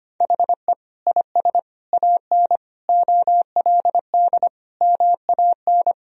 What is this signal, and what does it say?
Morse code, heard as a keyed tone, about 25 wpm, 700 Hz. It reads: HE IS AN OLD MAN
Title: Sentences from Top 1000 Words - Review Part 2 - 25wpm